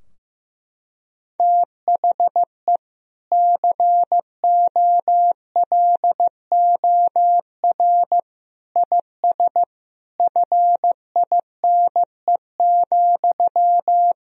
Morse code, heard as a keyed tone, about 15 wpm, 700 Hz. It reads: THE COLOR IS FINE,
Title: Sentences from Top 1000 Words - Review Part 2 - 15wpm